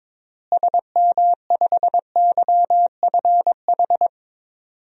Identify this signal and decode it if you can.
Morse code, heard as a keyed tone, about 21 wpm, 700 Hz. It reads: SM5YFH
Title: DXCC-100 Call Signs - 6 Characters - Repeated 2x - 22wpm